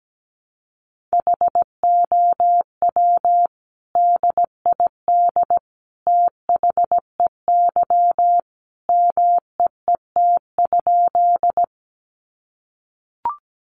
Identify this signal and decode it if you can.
Morse code, heard as a keyed tone, about 17 wpm, 700 Hz. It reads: HOW DID THEY MEET?
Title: Sentences from Top 1100 Words 17wpm